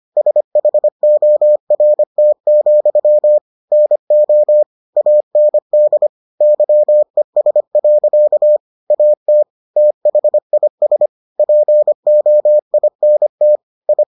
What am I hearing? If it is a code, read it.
Morse code, heard as a keyed tone, about 25 wpm, 600 Hz. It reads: SHORT, NO AND YES. AT THIS POINT I